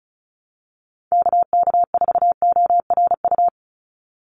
Morse code, heard as a keyed tone, about 34 wpm, 700 Hz. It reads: KK4ORU
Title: USA Call Signs 35wpm